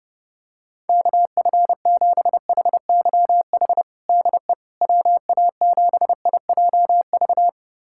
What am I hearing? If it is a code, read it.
Morse code, heard as a keyed tone, about 30 wpm, 700 Hz. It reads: KF7HYH DE WA7IJV
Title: USA Callsign DE Callsign - 6 Characters - 30wpm